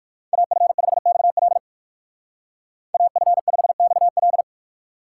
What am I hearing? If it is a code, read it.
Morse code, heard as a keyed tone, about 59 wpm, 700 Hz. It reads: AF5XL AF5XL